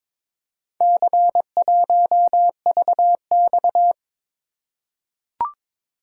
Morse code, heard as a keyed tone, about 22 wpm, 700 Hz. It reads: C1VX
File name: practice / 22wpm / Four Letter-Number Combinations - 22wpm